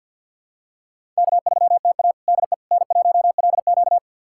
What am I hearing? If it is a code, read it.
Morse code, heard as a keyed tone, about 50 wpm, 700 Hz. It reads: K3TA DE N1LQ
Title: USA Callsign DE Callsign - 4 Characters - 50wpm